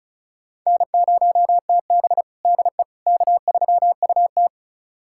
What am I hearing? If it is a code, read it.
Morse code, heard as a keyed tone, about 36 wpm, 700 Hz. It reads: N0TB DE K3UT
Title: USA Callsign DE Callsign - 4 Characters - 35wpm